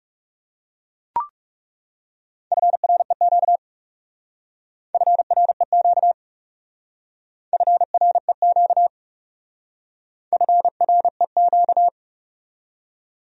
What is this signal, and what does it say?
Morse code, heard as a keyed tone, about 36 wpm, 700 Hz. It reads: FREQ FREQ FREQ FREQ
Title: Speed-Racing Abbreviations - 30wpm